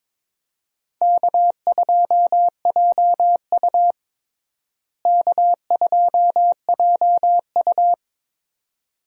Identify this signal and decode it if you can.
Morse code, heard as a keyed tone, about 22 wpm, 700 Hz. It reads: K2JU K2JU
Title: USA Call Signs - 4 Characters - Repeated 2x - 22wpm